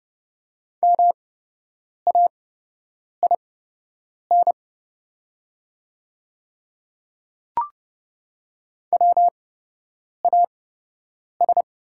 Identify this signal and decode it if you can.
Morse code, heard as a keyed tone, about 30 wpm, spaced as for 8 wpm, 700 Hz. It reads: MAIN WAS
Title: 8x Character Spacing - Top 300 Words - 30wpm